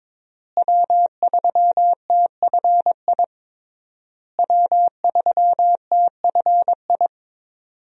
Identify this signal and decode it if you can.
Morse code, heard as a keyed tone, about 22 wpm, 700 Hz. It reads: W3TFI W3TFI
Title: USA Call Signs - 5 Characters - Repeated 3x - 22wpm